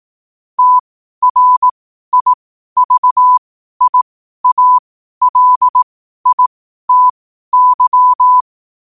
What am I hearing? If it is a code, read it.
Morse code, heard as a keyed tone, about 18 wpm, spaced as for 12 wpm, 1000 Hz. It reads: TRIVIALITY
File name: qotdfiles / 12wpm